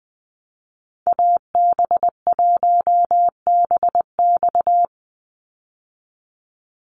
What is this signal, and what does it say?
Morse code, heard as a keyed tone, about 20 wpm, 700 Hz. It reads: AB1BX
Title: Call Signs Repeated 3x - 20wpm